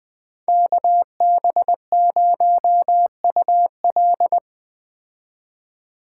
Morse code, heard as a keyed tone, about 21 wpm, 700 Hz. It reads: KB0UL